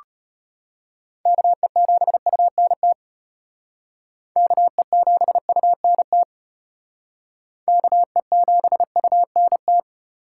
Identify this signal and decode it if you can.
Morse code, heard as a keyed tone, about 32 wpm, 700 Hz. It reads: KE7UNT KE7UNT KE7UNT